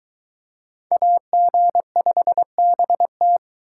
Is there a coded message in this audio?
Yes: Morse code from 0.9 to 3.4 s, heard as a keyed tone, about 22 wpm, 700 Hz. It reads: AG5BT